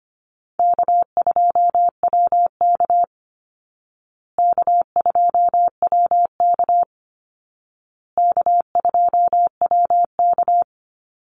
Morse code, heard as a keyed tone, about 25 wpm, 700 Hz. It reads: K2WK K2WK K2WK
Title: Easy Call Signs Repeated 3x - 25wpm